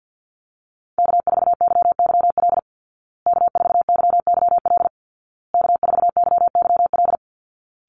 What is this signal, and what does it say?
Morse code, heard as a keyed tone, about 50 wpm, 700 Hz. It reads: K4YYL K4YYL K4YYL
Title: Call Signs Repeated 2x - 50wpm